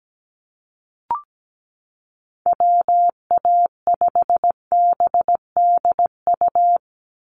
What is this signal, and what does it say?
Morse code, heard as a keyed tone, about 17 wpm, 700 Hz. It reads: WA5BDU
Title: Call Signs Repeated 2x - 17wpm